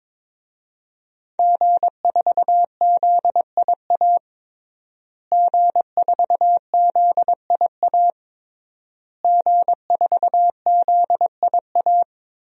Morse code, heard as a keyed tone, about 22 wpm, 700 Hz. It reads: G4ZIA G4ZIA G4ZIA
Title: DXCC-100 Call Signs - 5 Characters - Repeated 3x - 22wpm